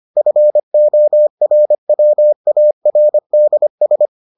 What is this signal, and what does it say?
Morse code, heard as a keyed tone, about 25 wpm, 600 Hz. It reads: FORWARDS